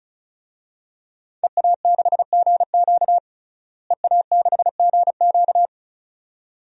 Morse code, heard as a keyed tone, about 34 wpm, 700 Hz. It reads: EA6GQ EA6GQ
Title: DXCC-100 Call Signs - 5 Characters - Repeated 2x - 35wpm